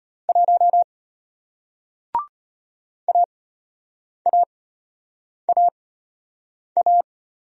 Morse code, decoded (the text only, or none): 1 AAAA